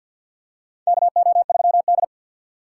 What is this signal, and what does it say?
Morse code, heard as a keyed tone, about 52 wpm, 700 Hz. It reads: KO3D
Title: USA Call Signs - 3x Character Spacing - 50wpm